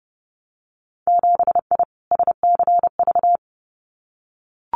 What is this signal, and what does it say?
Morse code, heard as a keyed tone, about 30 wpm, 700 Hz. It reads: 7I SCV
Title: Winter Field Day Exchanges Encore 30wpm